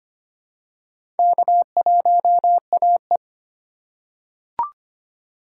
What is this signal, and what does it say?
Morse code, heard as a keyed tone, about 25 wpm, 700 Hz. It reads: K1AE
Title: USA Call Signs - 4 Characters - 4x Character Spacing - 25wpm